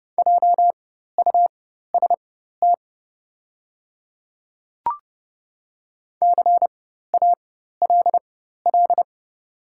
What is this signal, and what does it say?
Morse code, heard as a keyed tone, about 30 wpm, spaced as for 14 wpm, 700 Hz. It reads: JUST CALL
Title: 4x Character Spacing - Top 300 Words - 30wpm